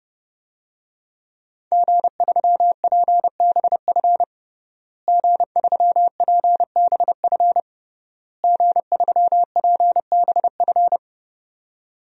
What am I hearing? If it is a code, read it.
Morse code, heard as a keyed tone, about 32 wpm, 700 Hz. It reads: G3PBF G3PBF G3PBF
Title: DXCC-100 Call Signs - 5 Characters - Repeated 3x - 30wpm